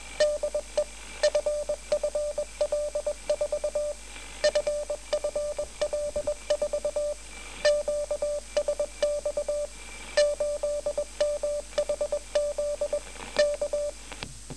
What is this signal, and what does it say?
Morse code, heard as a keyed tone, about 21 wpm, 590 Hz, with radio noise. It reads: DE FFL4 FFL4 QSX 8MHZ K
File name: FFL4 8522.5 1253